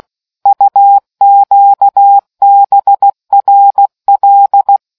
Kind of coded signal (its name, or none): Morse code